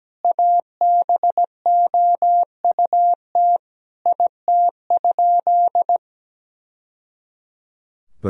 Morse code, heard as a keyed tone, about 17 wpm, 700 Hz. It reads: ABOUT IT?
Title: Sentences from Top 1000 Words - Review Part 4 - 17wpm